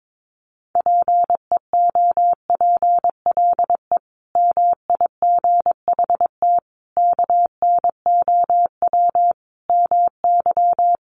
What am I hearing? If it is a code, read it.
Morse code, heard as a keyed tone, about 22 wpm, 700 Hz. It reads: PEOPLE MIGHT KNOW MY
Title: Sentences from Top 500 Words 22wpm